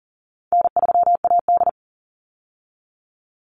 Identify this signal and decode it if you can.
Morse code, heard as a keyed tone, about 40 wpm, 700 Hz. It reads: N3AD
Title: Call Signs - 40wpm